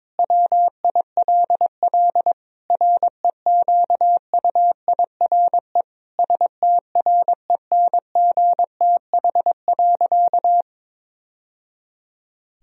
Morse code, heard as a keyed tone, about 22 wpm, 700 Hz. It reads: WILL REQUIRE STRENGTH.
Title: Sentences from Top 1000 Words - Review Part 4 - 22wpm